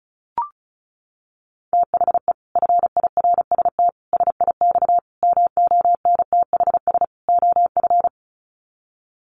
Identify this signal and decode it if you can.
Morse code, heard as a keyed tone, about 34 wpm, 700 Hz. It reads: THE FIRST SIX MONTHS OF
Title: Sets of 5 Words Encore2 35wpm